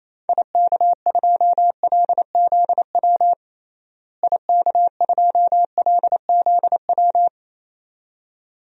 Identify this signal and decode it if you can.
Morse code, heard as a keyed tone, about 27 wpm, 700 Hz. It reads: IK2LZW IK2LZW
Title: DXCC-100 Call Signs - 6 Characters - Repeated 2x - 28wpm